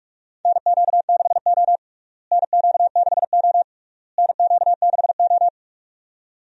Morse code, heard as a keyed tone, about 43 wpm, 700 Hz. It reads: NQ6O NQ6O NQ6O
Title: USA Call Signs - 4 Characters - Repeated 3x - 45wpm